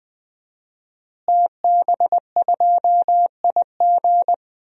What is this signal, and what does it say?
Morse code, heard as a keyed tone, about 20 wpm, 700 Hz. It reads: TB2IG
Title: DXCC-100 Call Signs - 5 Characters - Repeated 3x - 20wpm